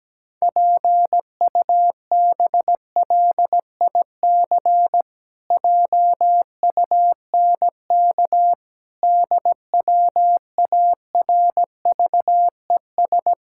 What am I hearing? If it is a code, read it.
Morse code, heard as a keyed tone, about 17 wpm, 700 Hz. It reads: PUBLIC JUNK DWARVES
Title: Pangram Challenge v2 - 17wpm